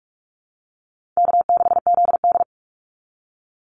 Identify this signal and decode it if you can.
Morse code, heard as a keyed tone, about 44 wpm, 700 Hz. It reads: K6ZD
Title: Call Signs Repeated 2x - 45wpm